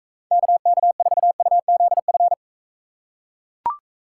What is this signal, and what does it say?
Morse code, heard as a keyed tone, about 39 wpm, 700 Hz. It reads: KK4UZF